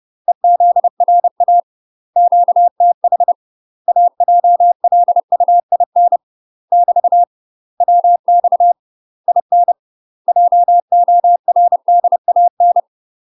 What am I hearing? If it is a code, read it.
Morse code, heard as a keyed tone, about 30 wpm, 700 Hz. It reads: EZRA QTH AJLUIN <BT> WX IN JORDAN
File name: UK MASTER QSOs TRAINING/30WPM